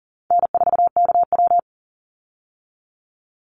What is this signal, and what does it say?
Morse code, heard as a keyed tone, about 40 wpm, 700 Hz. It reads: N4KW
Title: Easy Call Signs Repeated 3x - 40wpm